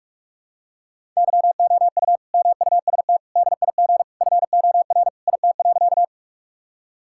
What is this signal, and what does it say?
Morse code, heard as a keyed tone, about 42 wpm, 700 Hz. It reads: YOU MUST DIG FOR IT.
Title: Sentences from Top 1600 Words 45wpm